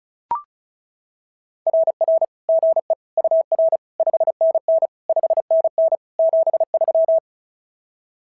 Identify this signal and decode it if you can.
Morse code, heard as a keyed tone, about 33 wpm, 660 Hz. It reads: RR GE UR 5NN 5NN 73
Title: SOTA QSO Practice 35wpm